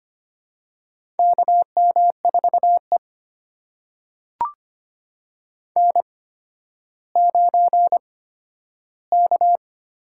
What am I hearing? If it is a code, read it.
Morse code, heard as a keyed tone, about 25 wpm, spaced as for 12 wpm, 700 Hz. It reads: KM4E N9K
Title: USA Call Signs - 4 Characters - 8x Character Spacing - 25wpm